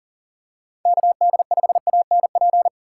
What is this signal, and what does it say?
Morse code, heard as a keyed tone, about 38 wpm, 700 Hz. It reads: KD5ANP
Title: USA Call Signs - 6 Characters - 6x Character Spacing - 40wpm